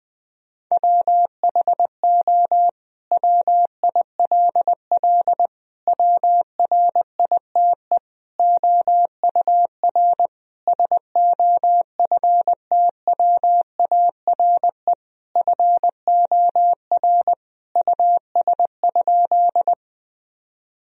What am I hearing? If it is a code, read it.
Morse code, heard as a keyed tone, about 20 wpm, 700 Hz. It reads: WHO WILL WRITE OUR SOFTWARE FOR US?
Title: Sentences from Top 1000 Words - Review Part 3 - 20wpm